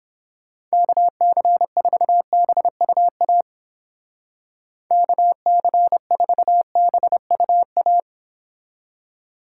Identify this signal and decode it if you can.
Morse code, heard as a keyed tone, about 29 wpm, 700 Hz. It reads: KC4BUA KC4BUA